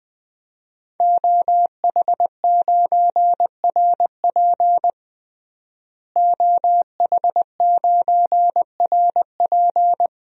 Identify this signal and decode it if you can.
Morse code, heard as a keyed tone, about 20 wpm, 700 Hz. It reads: OH9RP OH9RP
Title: DXCC-100 Call Signs - 5 Characters - Repeated 2x - 20wpm